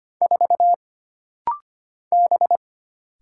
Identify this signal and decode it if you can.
Morse code, heard as a keyed tone, about 25 wpm, 700 Hz. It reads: 4 B
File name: Single Character (Letter:Number:Punctuation:Prosign) - Rapid-Fire - 25wpm